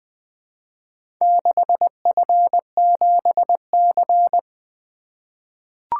Morse code, heard as a keyed tone, about 20 wpm, 700 Hz. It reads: 6F7C
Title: Four Letter-Number Combinations - 20wpm